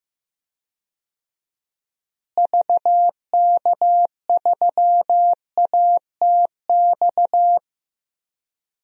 Morse code, heard as a keyed tone, about 15 wpm, 700 Hz. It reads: VK3ATX
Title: DXCC-100 Call Signs - 6 Characters - Repeated 3x - 15wpm